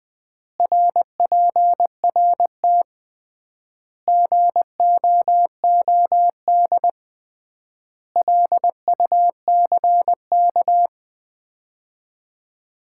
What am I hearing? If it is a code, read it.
Morse code, heard as a keyed tone, about 20 wpm, 700 Hz. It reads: RPRT GOOD LUCK
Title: Common QSO Phrases - 3x Word-Spacing - 20wpm